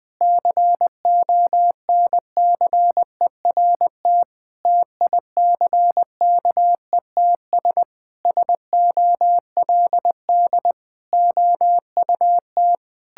Morse code, heard as a keyed tone, about 21 wpm, 700 Hz. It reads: CONCERT TICKETS SOLD OUT
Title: Sentences with 6-Word Phrases - 20wpm